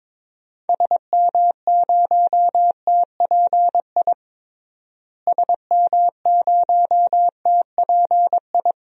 Morse code, heard as a keyed tone, about 22 wpm, 700 Hz. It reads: SM0TPI SM0TPI
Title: DXCC-100 Call Signs - 6 Characters - Repeated 2x - 22wpm